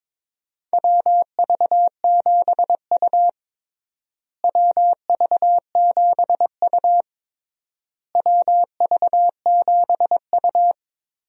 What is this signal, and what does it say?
Morse code, heard as a keyed tone, about 22 wpm, 700 Hz. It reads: WV7U WV7U WV7U
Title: USA Call Signs - 4 Characters - Repeated 3x - 22wpm